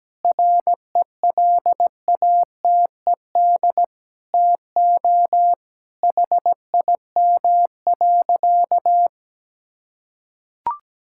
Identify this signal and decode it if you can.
Morse code, heard as a keyed tone, about 17 wpm, 700 Hz. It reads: RELATED TO HIM.
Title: Sentences from Top 1000 Words - Review Part 5 - 17wpm